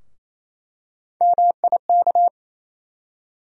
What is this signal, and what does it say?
Morse code, heard as a keyed tone, about 28 wpm, 700 Hz. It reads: MIK
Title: USA Call Sign Postfixes - 3 Letters 28wpm